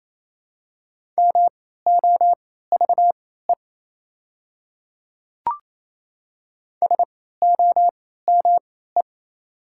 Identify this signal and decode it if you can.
Morse code, heard as a keyed tone, about 28 wpm, spaced as for 16 wpm, 700 Hz. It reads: MOVE SOME
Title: 3x Character Spacing - Top 500 Words - 28wpm